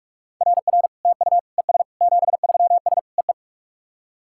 Morse code, heard as a keyed tone, about 46 wpm, 690 Hz. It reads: RR TU ES 73S EE
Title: SOTA QSO Practice 45wpm